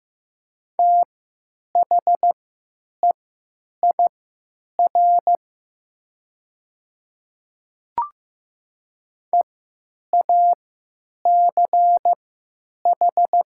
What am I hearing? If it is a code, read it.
Morse code, heard as a keyed tone, about 15 wpm, spaced as for 9 wpm, 700 Hz. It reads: THEIR EACH
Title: 3x Character Spacing - Top 100 Words - 15wpm